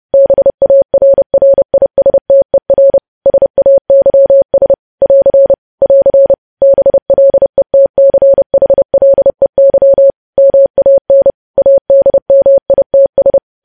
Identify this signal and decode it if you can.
Morse code, heard as a keyed tone, about 30 wpm, 550 Hz. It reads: BARRISTER SAYS <AR> <AR> BLETCHLEY MAN ADMITS
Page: news headlines in morse code at 30 wpm